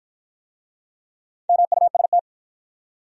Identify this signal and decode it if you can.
Morse code, heard as a keyed tone, about 53 wpm, 700 Hz. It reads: MUST